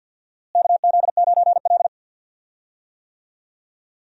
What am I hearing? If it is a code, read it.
Morse code, heard as a keyed tone, about 53 wpm, 700 Hz. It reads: KZ9L
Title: USA Call Signs - 4 Characters - Repeated 2x - 50wpm